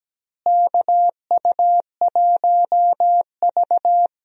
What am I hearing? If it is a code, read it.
Morse code, heard as a keyed tone, about 17 wpm, 700 Hz. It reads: KU1V
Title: USA Call Signs - 4 Characters - Repeated 3x - 17wpm